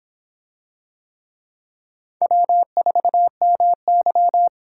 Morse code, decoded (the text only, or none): W4MY